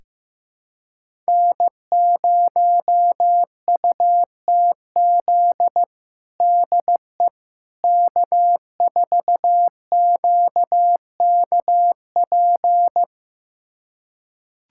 Morse code, heard as a keyed tone, about 16 wpm, 700 Hz. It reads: N0UTZ DE K4QKP
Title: USA Callsign DE Callsign - 5 Characters - 15wpm